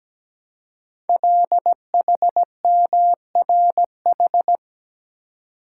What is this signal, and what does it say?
Morse code, heard as a keyed tone, about 17 wpm, 700 Hz. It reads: LHMRH
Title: Code Groups - 5 Letters - 17wpm